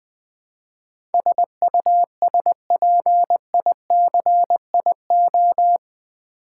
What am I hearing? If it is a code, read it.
Morse code, heard as a keyed tone, about 20 wpm, 700 Hz. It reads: SUSPICIO
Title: Build A Word - 4 Letters At A Time - 10-Letter Words - 20wpm